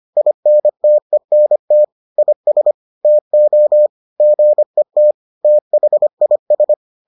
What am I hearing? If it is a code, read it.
Morse code, heard as a keyed tone, about 25 wpm, 600 Hz. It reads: INTENT IS TO GET THIS